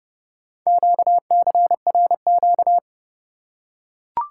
Morse code, heard as a keyed tone, about 28 wpm, 700 Hz. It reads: QCRQ